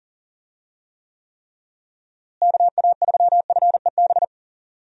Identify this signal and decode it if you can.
Morse code, heard as a keyed tone, about 40 wpm, 700 Hz. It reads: KA3FEB
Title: USA Call Signs - 6 Characters - Repeated 3x - 40wpm